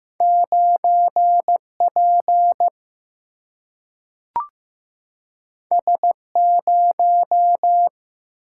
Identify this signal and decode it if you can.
Morse code, heard as a keyed tone, about 15 wpm, 700 Hz. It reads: 9 P S 0
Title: Single-Letter Single-Number 15wpm